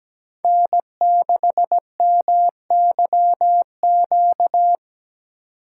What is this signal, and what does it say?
Morse code, heard as a keyed tone, about 20 wpm, 700 Hz. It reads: N6MYQ
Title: Speed Racing - USA Call Signs 17wpm